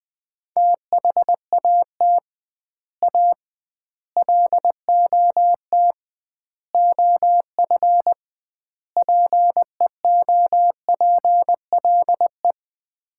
Morse code, heard as a keyed tone, about 20 wpm, 700 Hz. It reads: THAT A LOT OF PEOPLE
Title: 2x Word-Spacing - Sets of 5 Words - 20wpm